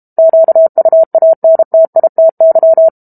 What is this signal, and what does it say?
Morse code, heard as a keyed tone, about 32 wpm, 650 Hz. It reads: QUANTITY
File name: Book Two / 06-Chapter 6 - 32 WPM